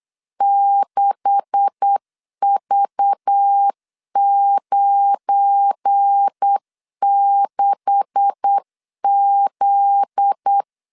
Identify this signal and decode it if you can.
Morse code, heard as a keyed tone, about 8 wpm, 800 Hz. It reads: 6V96Z
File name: LEZIONI / 9 286 8wpm 3425